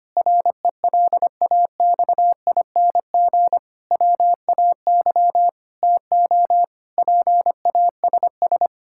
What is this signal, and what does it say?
Morse code, heard as a keyed tone, about 25 wpm, 700 Hz. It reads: RELAXING WAY TO PASS
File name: practice / 25wpm / Difficult Sentences with Hints - 25wpm